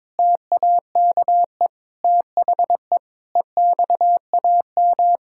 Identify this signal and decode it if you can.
Morse code, heard as a keyed tone, about 22 wpm, 700 Hz. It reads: TAKE THE EXAM
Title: Sentences from Top 1000 Words - Review Part 5 - 22wpm